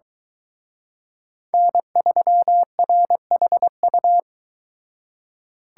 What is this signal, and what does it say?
Morse code, heard as a keyed tone, about 21 wpm, 700 Hz. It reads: N3RHU